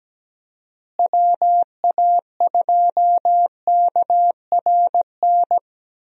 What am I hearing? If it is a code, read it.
Morse code, heard as a keyed tone, about 17 wpm, 700 Hz. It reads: WA2KRN
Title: USA Call Signs - 6 Characters - Repeated 2x - 17wpm